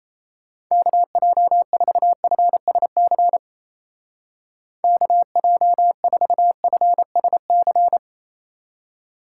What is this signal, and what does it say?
Morse code, heard as a keyed tone, about 34 wpm, 700 Hz. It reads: KJ4FSC KJ4FSC
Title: Speed Racing - USA 2x3 Call Signs 28wpm